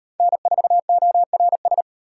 Morse code, heard as a keyed tone, about 33 wpm, 700 Hz. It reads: N4ORS